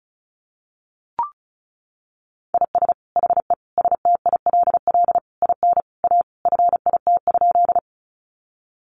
Morse code, heard as a keyed tone, about 34 wpm, 700 Hz. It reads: IS HE STILL IN A FIT?
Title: Sentences from Top 800 Words 35wpm